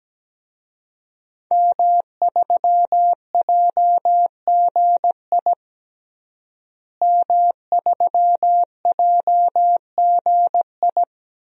Morse code, heard as a keyed tone, about 17 wpm, 700 Hz. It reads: M3JGI M3JGI
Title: DXCC-100 Call Signs - 5 Characters - Repeated 2x - 17wpm